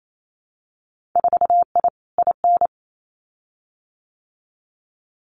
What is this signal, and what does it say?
Morse code, heard as a keyed tone, about 28 wpm, 700 Hz. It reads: 4I IN
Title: Winter Field Day Exchanges Encore 28wpm